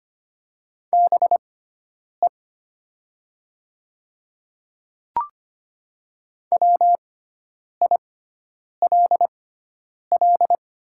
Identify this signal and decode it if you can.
Morse code, heard as a keyed tone, about 25 wpm, spaced as for 9 wpm, 700 Hz. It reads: BE WILL
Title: 6x Character Spacing - Top 100 Words - 25wpm